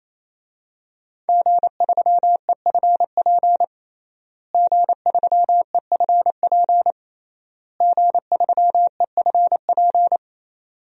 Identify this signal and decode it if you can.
Morse code, heard as a keyed tone, about 28 wpm, 700 Hz. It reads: G3EFP G3EFP G3EFP
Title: DXCC-100 Call Signs - 5 Characters - Repeated 3x - 28wpm